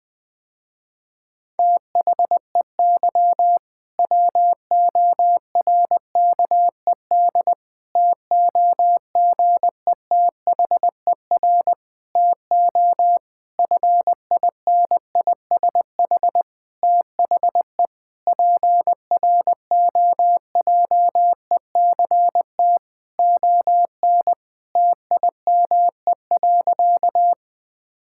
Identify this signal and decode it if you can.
Morse code, heard as a keyed tone, about 21 wpm, 700 Hz. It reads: THEY WORKED TOGETHER TO FINISH THE PROJECT ON TIME.
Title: Difficult Sentences with Hints - 20wpm